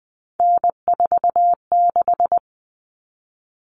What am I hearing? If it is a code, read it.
Morse code, heard as a keyed tone, about 20 wpm, 700 Hz. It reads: N46
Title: Common Cut Numbers - 3-Digit 20wpm